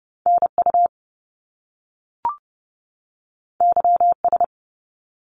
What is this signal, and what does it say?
Morse code, heard as a keyed tone, about 30 wpm, 700 Hz. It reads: NU YS
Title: Top Two Letter Combinations in Words 30wpm